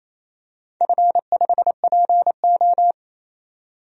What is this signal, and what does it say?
Morse code, heard as a keyed tone, about 28 wpm, 700 Hz. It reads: F5PO
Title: DXCC-100 Call Signs - 4 Characters - 28wpm